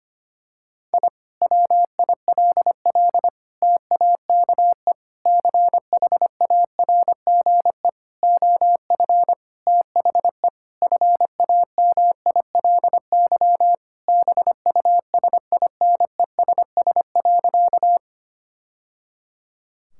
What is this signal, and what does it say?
Morse code, heard as a keyed tone, about 25 wpm, 700 Hz. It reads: I WILL TAKE CHARGE OF THE FAMILY BUSINESS.
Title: Sentences from Top 1000 Words - Review Part 6 - 25wpm